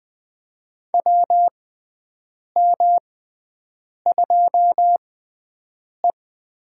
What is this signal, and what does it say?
Morse code, heard as a keyed tone, about 21 wpm, spaced as for 7 wpm, 700 Hz. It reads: WM2E
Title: USA Call Signs - 4 Characters - 6x Character Spacing - 20wpm